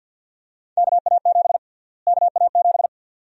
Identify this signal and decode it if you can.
Morse code, heard as a keyed tone, about 51 wpm, 700 Hz. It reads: KA7 KA7
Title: USA Call Sign Prefixes - Repeated 2x - 50wpm